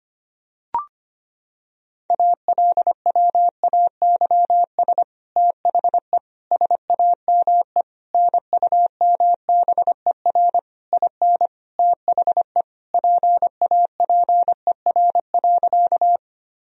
Morse code, heard as a keyed tone, about 25 wpm, 700 Hz. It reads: ALWAYS THE SAME NUMBER IN THE PAPER.
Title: Sentences from Top 1000 Words - Review Part 1 - 25wpm